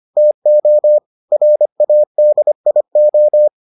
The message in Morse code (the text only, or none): TO RADIO